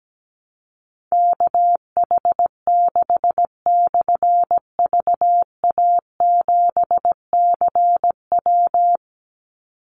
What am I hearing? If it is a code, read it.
Morse code, heard as a keyed tone, about 17 wpm, 700 Hz. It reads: KH6/VA7CW
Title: Call Signs - 17wpm